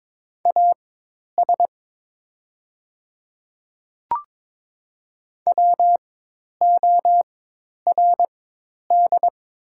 Morse code, heard as a keyed tone, about 22 wpm, spaced as for 10 wpm, 700 Hz. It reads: AS WORD
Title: 4x Character Spacing - Top 100 Words - 22wpm